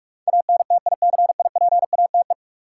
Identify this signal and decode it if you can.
Morse code, heard as a keyed tone, about 46 wpm, 700 Hz. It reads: ANTICIPATE
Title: Sentences from Top 1500 Words - New Word Drill - 45wpm